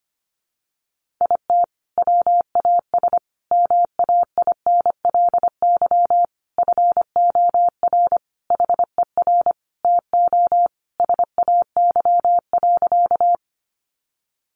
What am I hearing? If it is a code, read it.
Morse code, heard as a keyed tone, about 25 wpm, 700 Hz. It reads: IT WAS MAINLY FOR HER TO SAY.
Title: Sentences from Top 1400 Words 25wpm